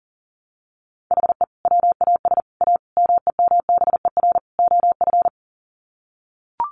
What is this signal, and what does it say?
Morse code, heard as a keyed tone, about 37 wpm, 700 Hz. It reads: HE WAS A MEMBER OF
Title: Sets of 5 Words Encore2 40wpm